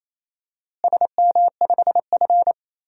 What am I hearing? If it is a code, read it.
Morse code, heard as a keyed tone, about 28 wpm, 700 Hz. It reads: SM5F
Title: DXCC-100 Call Signs - 4 Characters - Repeated 2x - 28wpm